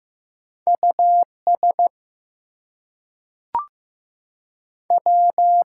Morse code, heard as a keyed tone, about 15 wpm, 700 Hz. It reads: US W
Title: Two Letter Words - 15wpm